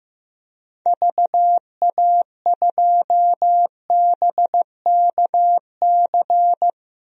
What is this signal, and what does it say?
Morse code, heard as a keyed tone, about 15 wpm, 700 Hz. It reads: VA2BKC
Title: DXCC-100 Call Signs - 6 Characters - 15wpm